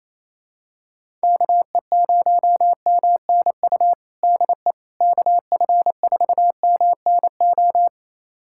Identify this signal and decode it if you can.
Morse code, heard as a keyed tone, about 28 wpm, 700 Hz. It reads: KE0MNU DE KF4MNO
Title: USA Callsign DE Callsign - 6 Characters - 28wpm